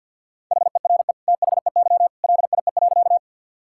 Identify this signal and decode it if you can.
Morse code, heard as a keyed tone, about 50 wpm, 700 Hz. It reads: HERE THEY LIE.
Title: Sentences from Top 1600 Words 50wpm